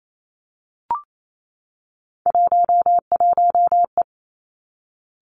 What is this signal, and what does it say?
Morse code, heard as a keyed tone, about 28 wpm, 690 Hz. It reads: 11E
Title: Three Challenging Letter-Number Combinations 28wpm